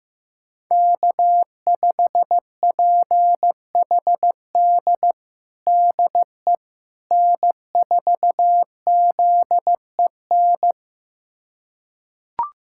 Morse code, heard as a keyed tone, about 15 wpm, 700 Hz. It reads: K5PHD DE N4ZEN
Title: USA Callsign DE Callsign - 5 Characters - 15wpm